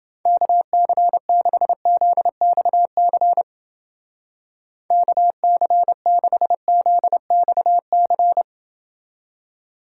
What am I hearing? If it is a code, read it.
Morse code, heard as a keyed tone, about 29 wpm, 700 Hz. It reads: KC6ZXC KC6ZXC